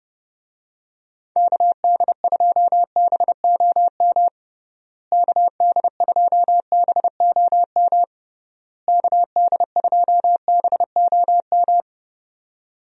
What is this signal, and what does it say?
Morse code, heard as a keyed tone, about 30 wpm, 700 Hz. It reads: KD2BOM KD2BOM KD2BOM
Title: USA Call Signs - 6 Characters - Repeated 3x - 30wpm